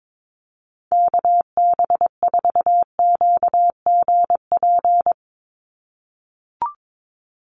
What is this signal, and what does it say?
Morse code, heard as a keyed tone, about 22 wpm, 700 Hz. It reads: KB4QGP
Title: USA Call Signs 22wpm